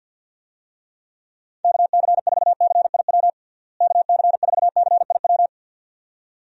Morse code, heard as a keyed tone, about 53 wpm, 700 Hz. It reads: KC4CIW KC4CIW
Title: USA Call Signs - 6 Characters - Repeated 2x - 50wpm